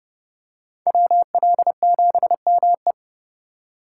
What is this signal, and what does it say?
Morse code, heard as a keyed tone, about 29 wpm, 700 Hz. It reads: WL7ME